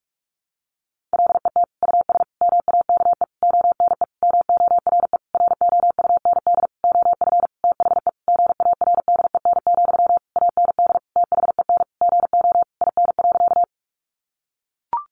Morse code, heard as a keyed tone, about 44 wpm, 700 Hz. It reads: LET US MAKE ONE MORE ROUND OF THE GARDEN, AND THEN GO IN.
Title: Sentences from Top 700 Words 45wpm